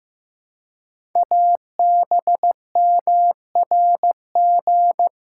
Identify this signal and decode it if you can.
Morse code, heard as a keyed tone, about 15 wpm, 700 Hz. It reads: ABMRG
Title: Code Groups - 5 Letters - 15wpm